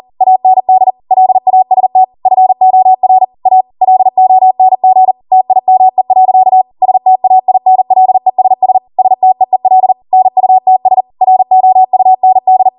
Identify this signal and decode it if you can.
Morse code, heard as a keyed tone, about 40 wpm, 750 Hz. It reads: AND LAST FOR A LONG TIME. STAINLESS STEEL NUTS ROUND